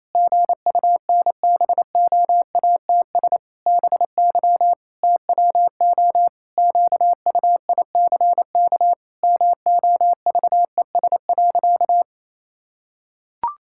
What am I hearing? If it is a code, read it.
Morse code, heard as a keyed tone, about 28 wpm, 700 Hz. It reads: GUNBOATS BY TWO QUICK MOVES.
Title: Pangram Challenge v2 - 28wpm